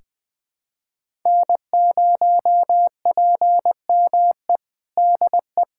Morse code, heard as a keyed tone, about 20 wpm, 700 Hz. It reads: N0PME DE
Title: USA Callsign DE Callsign - 5 Characters - 20wpm